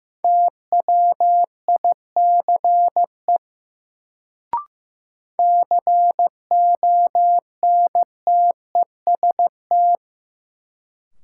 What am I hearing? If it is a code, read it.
Morse code, heard as a keyed tone, about 15 wpm, 700 Hz. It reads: TWICE CONTEST
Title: Sentences from Top 1600 Words - New Word Drill - 15wpm